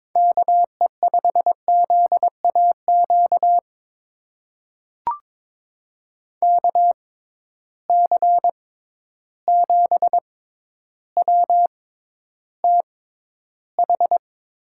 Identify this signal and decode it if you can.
Morse code, heard as a keyed tone, about 22 wpm, spaced as for 11 wpm, 700 Hz. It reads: KE5ZAQ KC7WTH
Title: USA Call Signs - 6 Characters - 6x Character Spacing - 22wpm